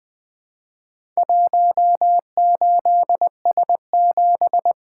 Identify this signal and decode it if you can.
Morse code, heard as a keyed tone, about 20 wpm, 700 Hz. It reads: 18S7
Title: Four Letter-Number Combinations - 20wpm